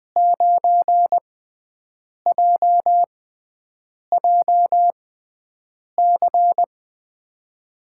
Morse code, heard as a keyed tone, about 20 wpm, spaced as for 7 wpm, 700 Hz. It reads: 9JJC
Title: USA Call Signs - 6 Characters - 6x Character Spacing - 20wpm